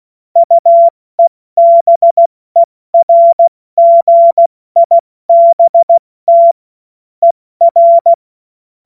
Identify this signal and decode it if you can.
Morse code, heard as a keyed tone, about 16 wpm, 680 Hz. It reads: UEBERGIBT ER
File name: Morsekurs / Witze / Witz 06. September 2022 70 F80